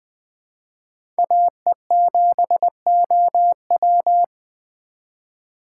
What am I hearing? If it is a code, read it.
Morse code, heard as a keyed tone, about 20 wpm, 700 Hz. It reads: AE7OW
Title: USA Call Signs - 5 Characters - 8x Character Spacing - 20wpm